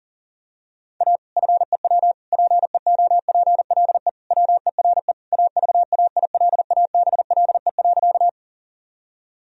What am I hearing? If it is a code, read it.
Morse code, heard as a keyed tone, about 40 wpm, 700 Hz. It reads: A FEW PEOPLE WERE AVAILABLE.
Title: Sentences from Top 1000 Words - Review Part 3 - 40wpm